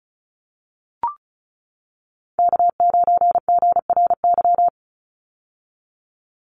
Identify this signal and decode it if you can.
Morse code, heard as a keyed tone, about 36 wpm, 700 Hz. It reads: K9GRY
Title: USA Call Signs 35wpm